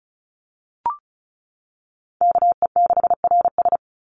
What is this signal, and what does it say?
Morse code, heard as a keyed tone, about 34 wpm, 700 Hz. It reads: KE6RS
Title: NA SOTA Chaser Callsigns 35wpm